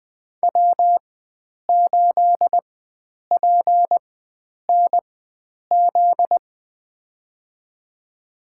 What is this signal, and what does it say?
Morse code, heard as a keyed tone, about 20 wpm, spaced as for 9 wpm, 700 Hz. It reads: W8PNZ
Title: USA Call Signs - 5 Characters - 4x Character Spacing - 20wpm